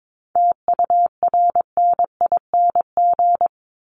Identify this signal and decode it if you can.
Morse code, heard as a keyed tone, about 22 wpm, 700 Hz. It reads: TURNING
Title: ING Suffix - Top 100 Words with Base Words 22wpm